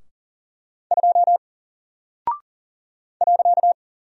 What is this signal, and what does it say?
Morse code, heard as a keyed tone, about 40 wpm, 700 Hz. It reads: 2 .
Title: Single Character (Letter:Number:Punctuation:Prosign) 40wpm